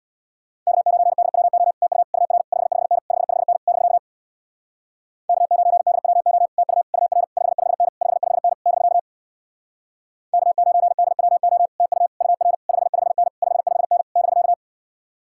Morse code, heard as a keyed tone, about 66 wpm, 700 Hz. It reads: K9DWO TU UR 55N 55N <BK> K9DWO TU UR 55N 55N <BK> K9DWO TU UR 55N 55N <BK>